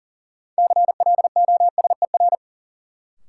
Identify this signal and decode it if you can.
Morse code, heard as a keyed tone, about 37 wpm, 700 Hz. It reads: CLOSER
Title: Sets of 5 Words Encore - New Word Drill - 40wpm